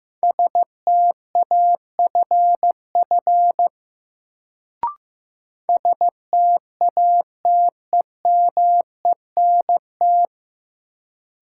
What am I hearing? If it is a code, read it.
Morse code, heard as a keyed tone, about 15 wpm, 700 Hz. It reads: STAFF STATEMENT
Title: Sentences from Top 1200 Words - New Word Drill Alt - 15wpm